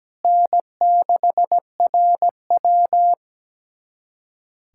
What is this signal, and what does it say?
Morse code, heard as a keyed tone, about 17 wpm, 700 Hz. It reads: N6RW
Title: USA Callsign DE Callsign - 4 Characters - 17wpm